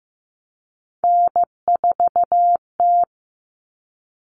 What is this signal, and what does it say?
Morse code, heard as a keyed tone, about 15 wpm, 700 Hz. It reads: N4T
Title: All Cut Numbers - 3-Digit 15wpm